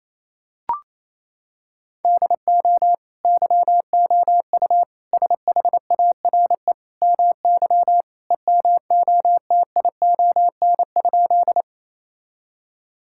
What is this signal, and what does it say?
Morse code, heard as a keyed tone, about 28 wpm, 700 Hz. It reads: DO YOU SHARE MY EMOTION?
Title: Sentences from Top 1600 Words 28wpm